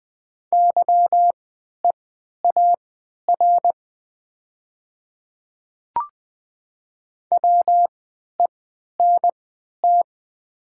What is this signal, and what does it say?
Morse code, heard as a keyed tone, about 20 wpm, spaced as for 11 wpm, 700 Hz. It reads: YEAR WENT
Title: 3x Character Spacing - Top 300 Words - 20wpm